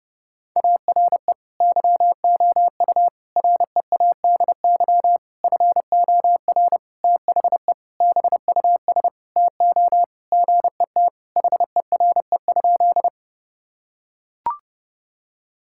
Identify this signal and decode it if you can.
Morse code, heard as a keyed tone, about 32 wpm, 700 Hz. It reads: ARE YOU READY FOR THE BUS TO GET HERE?
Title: Sentences from Top 1000 Words - Review Part 6 - 30wpm